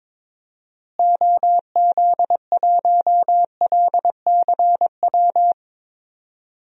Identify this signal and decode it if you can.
Morse code, heard as a keyed tone, about 22 wpm, 700 Hz. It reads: OZ1LCW
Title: DXCC-100 Call Signs - 6 Characters - Repeated 2x - 22wpm